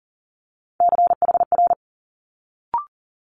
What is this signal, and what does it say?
Morse code, heard as a keyed tone, about 40 wpm, 700 Hz. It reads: CHR
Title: Multi-letter Phonemes Quick Fire 40wpm